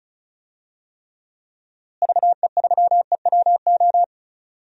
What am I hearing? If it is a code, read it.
Morse code, heard as a keyed tone, about 34 wpm, 700 Hz. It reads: VE3EWO
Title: DXCC-100 Call Signs - 6 Characters - Repeated 3x - 35wpm